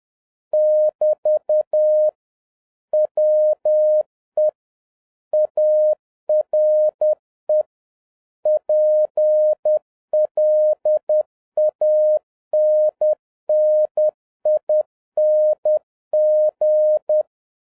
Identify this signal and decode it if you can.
Morse code, heard as a keyed tone, about 10 wpm, 610 Hz. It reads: <BT> WE ARE PLANNING